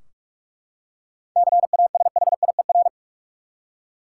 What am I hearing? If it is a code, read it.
Morse code, heard as a keyed tone, about 43 wpm, 700 Hz. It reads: CASHIER
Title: Practice QSO - Exchange 3 - More Info - 45wpm